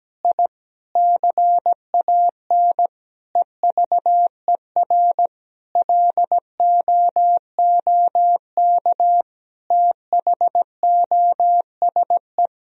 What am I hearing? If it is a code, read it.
Morse code, heard as a keyed tone, about 17 wpm, 700 Hz. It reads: I CAN EVER LOOK THOSE
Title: Sentences from Top 1000 Words - Review Part 2 - 17wpm